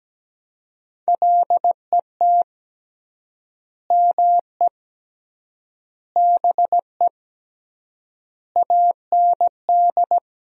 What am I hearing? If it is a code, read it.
Morse code, heard as a keyed tone, about 17 wpm, 700 Hz. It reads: LET ME BE AND
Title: 3x Word-Spacing - Sentences from Top 300 Words - 17wpm